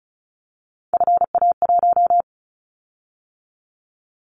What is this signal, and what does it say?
Morse code, heard as a keyed tone, about 36 wpm, 690 Hz. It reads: FA1
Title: Callsign Prefixes for Top 50 DXCC - 35wpm